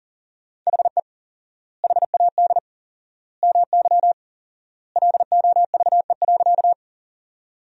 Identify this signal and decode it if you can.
Morse code, heard as a keyed tone, about 40 wpm, 700 Hz. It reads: HE HAD MY LOVE.
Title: 4x Word-Spacing - Sentences from Top 400 Words - 40wpm